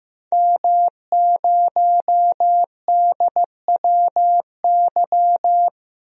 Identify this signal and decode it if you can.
Morse code, heard as a keyed tone, about 15 wpm, 700 Hz. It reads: M0DWY
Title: DXCC-100 Call Signs - 5 Characters - 15wpm